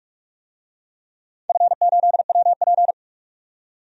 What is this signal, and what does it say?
Morse code, heard as a keyed tone, about 47 wpm, 700 Hz. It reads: F8WP
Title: DXCC-100 Call Signs - 4 Characters - 45wpm